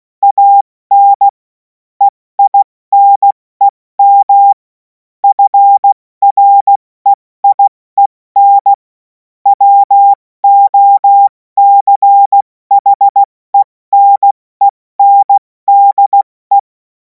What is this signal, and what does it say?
Morse code, heard as a keyed tone, about 16 wpm, 800 Hz. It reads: AN EINEM FREIEN WOCHENENDE
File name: Morsekurs / Witze / Witz 31. März 2026 70 F80